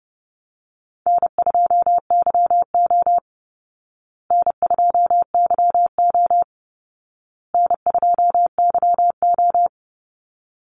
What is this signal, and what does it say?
Morse code, heard as a keyed tone, about 30 wpm, 700 Hz. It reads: N2YO N2YO N2YO
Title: Easy Call Signs Repeated 3x - 30wpm